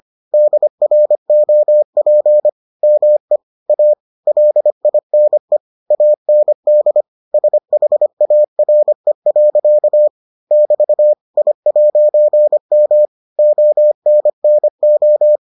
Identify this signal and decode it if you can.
Morse code, heard as a keyed tone, about 25 wpm, 600 Hz. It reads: DROP ME A LINE AND SHARE. <BT> I'M ONNO